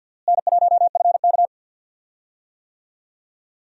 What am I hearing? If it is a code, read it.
Morse code, heard as a keyed tone, about 52 wpm, 700 Hz. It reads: N1FK
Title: USA Call Signs - 4 Characters - Repeated 3x - 50wpm